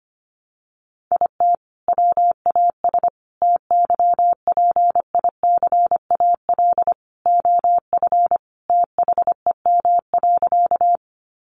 Morse code, heard as a keyed tone, about 25 wpm, 700 Hz. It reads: IT WAS TYPICAL OF THEM.
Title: Sentences from Top 1100 Words 25wpm